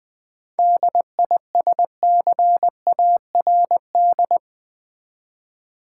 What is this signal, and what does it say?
Morse code, heard as a keyed tone, about 20 wpm, 700 Hz. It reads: DISCARD
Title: 7 Letter Words - First 3 Letters Given - 20wpm